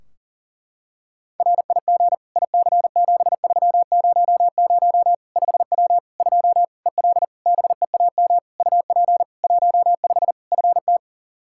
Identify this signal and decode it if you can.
Morse code, heard as a keyed tone, about 40 wpm, 700 Hz. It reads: RIG IC7300 5W 2 EL BEAM UP 15 FT
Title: K1USN SST QSO Practice 40wpm